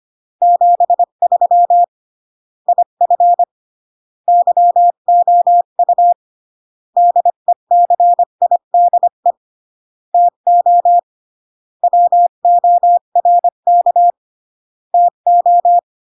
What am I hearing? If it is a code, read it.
Morse code, heard as a keyed tone, about 25 wpm, 700 Hz. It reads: 73 IF YOU DECIDE TO WORK TO